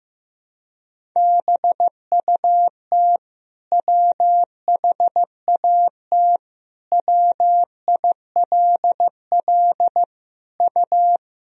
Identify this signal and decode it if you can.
Morse code, heard as a keyed tone, about 15 wpm, 700 Hz. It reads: BUT WHAT WILL U
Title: Sentences from Top 100 Words and Abbreviations 15wpm